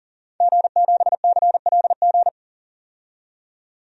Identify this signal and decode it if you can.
Morse code, heard as a keyed tone, about 40 wpm, 700 Hz. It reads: G7CLG
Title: DXCC-100 Call Signs - 5 Characters - 40wpm